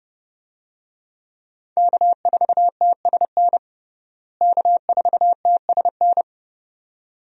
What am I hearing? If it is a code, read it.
Morse code, heard as a keyed tone, about 30 wpm, 700 Hz. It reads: K4TSN K4TSN
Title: USA Call Signs - 5 Characters - Repeated 2x - 30wpm